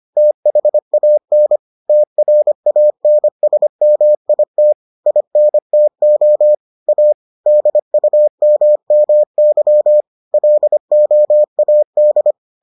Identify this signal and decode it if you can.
Morse code, heard as a keyed tone, about 25 wpm, 600 Hz. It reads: THAN TRANSMIT INTO A DUMMY LOAD